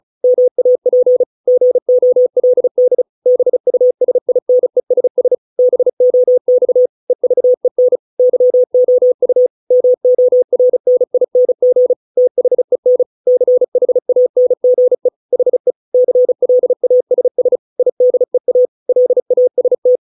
Morse code, heard as a keyed tone, about 34 wpm, 500 Hz. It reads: MAP GOLD BUSINESS BOX EVEN YOU MORNING THEN CHANGE HE CLASS IDEA LAST